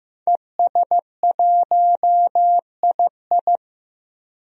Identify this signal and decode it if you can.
Morse code, heard as a keyed tone, about 15 wpm, 700 Hz. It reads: ES1II
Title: DXCC-100 Call Signs - 5 Characters - Repeated 3x - 15wpm